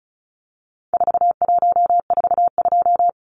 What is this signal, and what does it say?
Morse code, heard as a keyed tone, about 34 wpm, 700 Hz. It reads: 4142
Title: Numbers - 4-Digit 35wpm